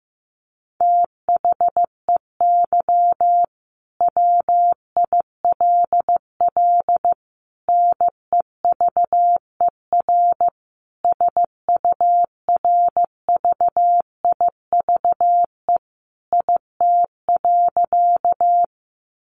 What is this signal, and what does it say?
Morse code, heard as a keyed tone, about 15 wpm, 700 Hz. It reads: THEY WILL NEVER SURVIVE IT.
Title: Sentences from Top 1300 Words 15wpm